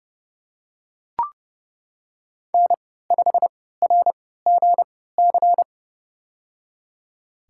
Morse code, heard as a keyed tone, about 30 wpm, spaced as for 17 wpm, 700 Hz. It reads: N5RGC
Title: USA Call Signs - 5 Characters - 3x Character Spacing - 30wpm